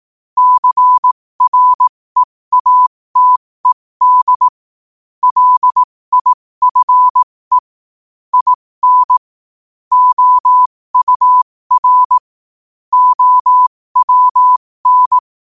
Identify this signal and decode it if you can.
Morse code, heard as a keyed tone, about 18 wpm, spaced as for 15 wpm, 1000 Hz. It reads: CREATED LIFE IN OUR OWN
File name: qotdfiles / 15wpm